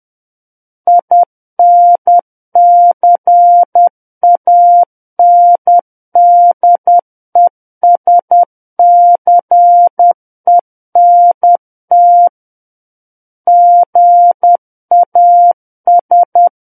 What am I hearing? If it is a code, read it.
Morse code, heard as a keyed tone, about 10 wpm, 700 Hz. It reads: INCANDESCENT GAS
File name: Book One / 01-Chapter 1 - 10 WPM